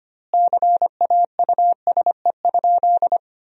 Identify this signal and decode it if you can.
Morse code, heard as a keyed tone, about 25 wpm, 700 Hz. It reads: CAUSE?
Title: 2x Word-Spacing - Sentences from Top 200 Words - 25wpm